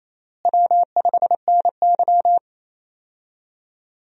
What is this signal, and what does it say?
Morse code, heard as a keyed tone, about 28 wpm, 700 Hz. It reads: W5NY
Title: USA Call Signs - 4 Characters - Repeated 3x - 28wpm